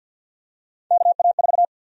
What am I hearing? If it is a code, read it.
Morse code, heard as a keyed tone, about 51 wpm, 700 Hz. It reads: KA4
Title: USA Call Sign Prefixes - 50wpm